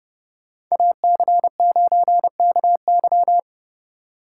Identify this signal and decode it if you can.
Morse code, heard as a keyed tone, about 39 wpm, 700 Hz. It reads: AC9KY